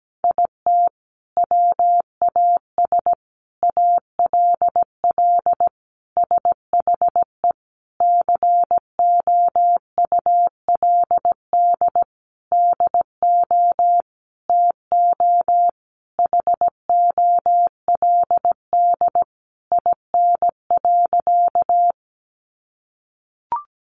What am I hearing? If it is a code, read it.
Morse code, heard as a keyed tone, about 17 wpm, 700 Hz. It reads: IT WAS ALL SHE COULD DO TO HOLD IN.
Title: Sentences from Top 400 Words 17wpm